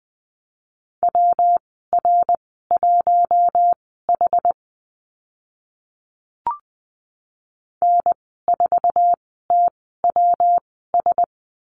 Morse code, heard as a keyed tone, about 20 wpm, spaced as for 14 wpm, 700 Hz. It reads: WR1H N4TWS
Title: 140 Lesson 23g - Letter f - Call Signs 20wpm